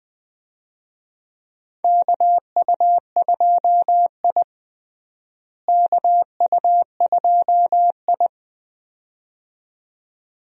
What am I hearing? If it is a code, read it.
Morse code, heard as a keyed tone, about 20 wpm, 700 Hz. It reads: KU2I KU2I
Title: USA Call Signs - 4 Characters - Repeated 2x - 20wpm